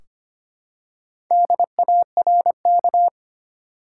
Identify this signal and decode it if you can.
Morse code, heard as a keyed tone, about 25 wpm, 700 Hz. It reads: DARK
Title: Four Letter Words - 25wpm